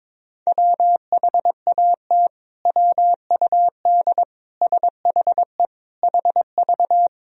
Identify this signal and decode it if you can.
Morse code, heard as a keyed tone, about 22 wpm, 700 Hz. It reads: WHAT WUD SHE HV
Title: Sentences from Top 100 Words and Abbreviations 22wpm